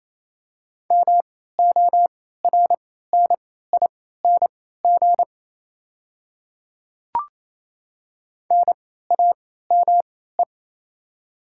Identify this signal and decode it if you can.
Morse code, heard as a keyed tone, about 28 wpm, spaced as for 16 wpm, 700 Hz. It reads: MORNING NAME
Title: 3x Character Spacing - Top 500 Words - 28wpm